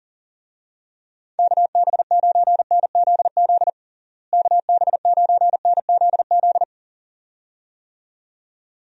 Morse code, heard as a keyed tone, about 40 wpm, 700 Hz. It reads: KB9NZZ KB9NZZ
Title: USA Call Signs - 6 Characters - Repeated 2x - 40wpm